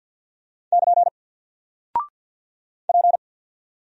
Morse code, heard as a keyed tone, about 50 wpm, 700 Hz. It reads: <KN> P
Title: Single Character (Letter:Number:Punctuation:Prosign) 50wpm